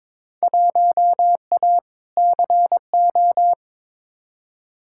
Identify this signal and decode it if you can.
Morse code, heard as a keyed tone, about 22 wpm, 700 Hz. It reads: 1A CO
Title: Field Day Exchanges - 22wpm